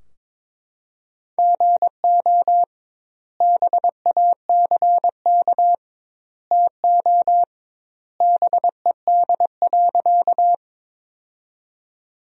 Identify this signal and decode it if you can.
Morse code, heard as a keyed tone, about 22 wpm, 700 Hz. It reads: GO BACK TO BED.
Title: 2x Word-Spacing - Sentences from Top 500 Words - 22wpm